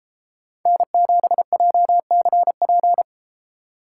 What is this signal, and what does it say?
Morse code, heard as a keyed tone, about 30 wpm, 700 Hz. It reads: N7JCP